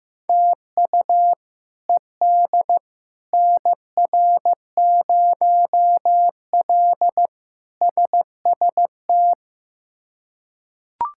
Morse code, heard as a keyed tone, about 15 wpm, 700 Hz. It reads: TU ED NR0L SST
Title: K1USN SST QSO Practice - 15wpm